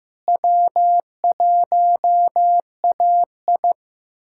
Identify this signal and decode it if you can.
Morse code, heard as a keyed tone, about 15 wpm, 700 Hz. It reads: W1AI
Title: USA Callsign DE Callsign - 4 Characters - 15wpm